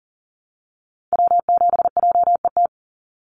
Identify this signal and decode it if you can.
Morse code, heard as a keyed tone, about 40 wpm, 700 Hz. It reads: W7JET
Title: NA SOTA Activator Callsigns 40wpm